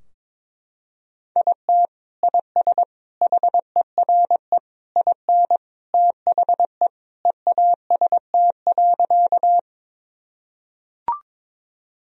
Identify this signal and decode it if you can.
Morse code, heard as a keyed tone, about 22 wpm, 700 Hz. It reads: IT IS HERE IN THE EAST.
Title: Sentences from Top 1000 Words - Review Part 3 - 22wpm